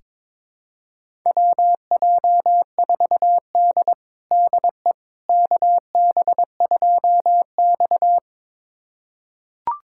Morse code, heard as a keyed tone, about 22 wpm, 700 Hz. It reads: WJ4D DE KB2X
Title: USA Callsign DE Callsign - 4 Characters - 22wpm